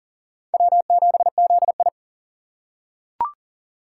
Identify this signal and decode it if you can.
Morse code, heard as a keyed tone, about 40 wpm, 700 Hz. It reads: W7ZI
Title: USA Call Signs - 4 Characters - 6x Character Spacing - 40wpm